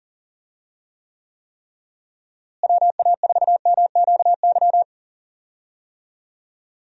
Morse code, heard as a keyed tone, about 40 wpm, 700 Hz. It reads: WA4MQY
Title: USA Call Signs - 6 Characters - Repeated 3x - 40wpm